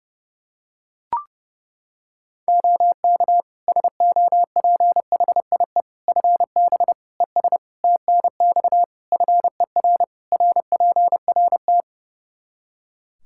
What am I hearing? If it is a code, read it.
Morse code, heard as a keyed tone, about 32 wpm, 700 Hz. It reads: OK SOPHIE FB ES TNX FER RPRT
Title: Practice QSO - Exchange 2 - Info - 30wpm